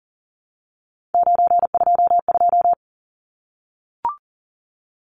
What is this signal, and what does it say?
Morse code, heard as a keyed tone, about 37 wpm, 700 Hz. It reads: 922
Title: Numbers - 3-Digit 40wpm